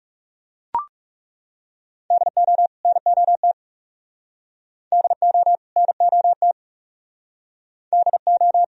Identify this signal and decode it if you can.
Morse code, heard as a keyed tone, about 38 wpm, 700 Hz. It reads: DO NOT DO NOT DO